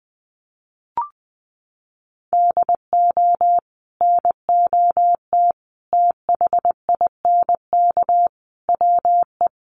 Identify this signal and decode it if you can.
Morse code, heard as a keyed tone, about 20 wpm, 700 Hz. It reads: DO NOT THINK WE
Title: Sets of 5 Words Encore 20wpm